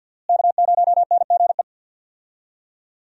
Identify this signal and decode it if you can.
Morse code, heard as a keyed tone, about 51 wpm, 700 Hz. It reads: K0NGE
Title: USA Call Signs - 5 Characters - Repeated 2x - 50wpm